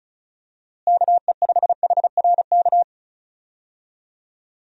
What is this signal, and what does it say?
Morse code, heard as a keyed tone, about 34 wpm, 700 Hz. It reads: KE5HRK
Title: USA Call Signs - 6 Characters - Repeated 2x - 35wpm